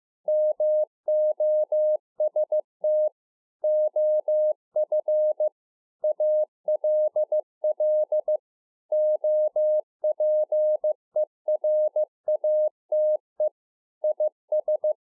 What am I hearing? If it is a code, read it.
Morse code, heard as a keyed tone, about 15 wpm, 610 Hz. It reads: MOST OF ALL OPERATE IS